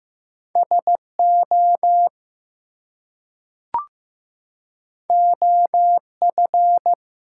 Morse code, heard as a keyed tone, about 15 wpm, 700 Hz. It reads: SO OF
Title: Two Letter Words - 15wpm